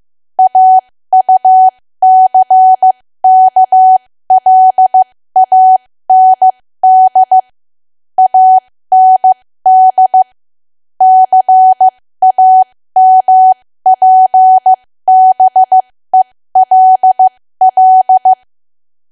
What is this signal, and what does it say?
Morse code, heard as a keyed tone, about 15 wpm, spaced as for 13 wpm, 750 Hz. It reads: AUCKLAND AND CAMPBELL
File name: cwa / W1AW-Level2 / 13WPM